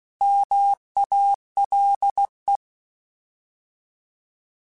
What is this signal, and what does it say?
Morse code, heard as a keyed tone, about 16 wpm, 800 Hz. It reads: MALE
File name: Alphabetical / Alphabetical mp3s / Ms @15